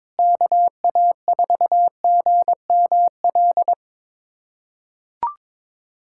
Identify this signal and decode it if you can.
Morse code, heard as a keyed tone, about 22 wpm, 700 Hz. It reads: KA4GML
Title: USA Callsign DE Callsign - 6 Characters - 22wpm